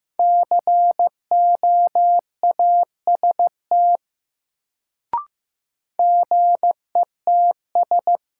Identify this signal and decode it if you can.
Morse code, heard as a keyed tone, about 15 wpm, 700 Hz. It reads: COAST GETS
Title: Sentences from Top 1300 Words - New Word Drill Alt - 15wpm